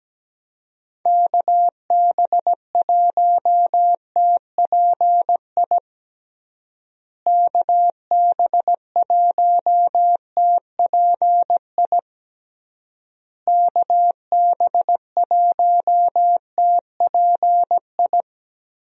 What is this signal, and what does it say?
Morse code, heard as a keyed tone, about 17 wpm, 700 Hz. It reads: KB1TPI KB1TPI KB1TPI
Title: USA Call Signs - 6 Characters - Repeated 3x - 17wpm